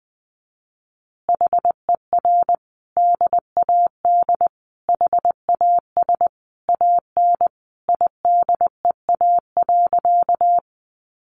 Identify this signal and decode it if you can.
Morse code, heard as a keyed tone, about 20 wpm, 700 Hz. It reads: HER DAD HAS AN IDEA.
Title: Additional Easy Phrases 20wpm